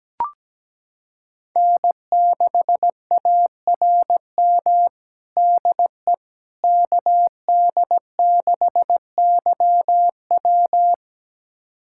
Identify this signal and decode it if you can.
Morse code, heard as a keyed tone, about 17 wpm, 700 Hz. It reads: N6ARM DE KD6YW
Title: USA Callsign DE Callsign - 5 Characters - 17wpm